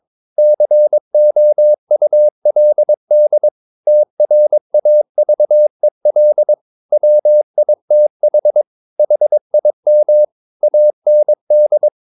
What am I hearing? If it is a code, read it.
Morse code, heard as a keyed tone, about 22 wpm, 600 Hz. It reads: COULD TRAVEL WITH HIM AND
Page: Romanen Never af Ken Follet på engelsk.